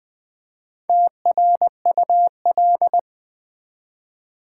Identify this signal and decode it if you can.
Morse code, heard as a keyed tone, about 20 wpm, 700 Hz. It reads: TRUL
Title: Build A Word - 1 Letter At A Time - 3 to 5-Letter Words - 20wpm